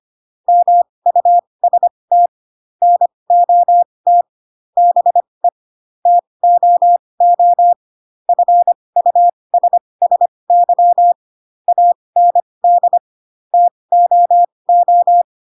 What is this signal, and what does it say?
Morse code, heard as a keyed tone, about 25 wpm, spaced as for 20 wpm, 700 Hz. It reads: MUST NOT BE TOO FUSSY AND TOO
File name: Pinocchio / Pinocchio ch7 15 25 ik1hge